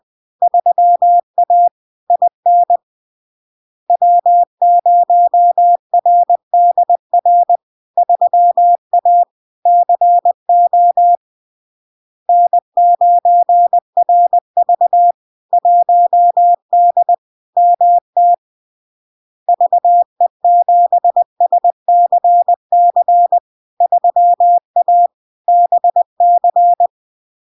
Morse code, heard as a keyed tone, about 20 wpm, 700 Hz. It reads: 3A IN W0RDR 3A CO N9RV 1D MT VE7SCC 3A BC
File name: LARK-FD-2022-Log-in-CW-20wpm-700-Hz